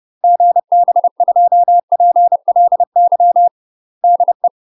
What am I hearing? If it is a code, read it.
Morse code, heard as a keyed tone, about 30 wpm, 700 Hz. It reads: GB2PLY DE
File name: UK MASTER QSOs TRAINING / 30WPM / UK MASTER QSOs 0018